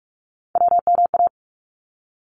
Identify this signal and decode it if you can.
Morse code, heard as a keyed tone, about 44 wpm, 700 Hz. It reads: WMA
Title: ARRL Sections 45wpm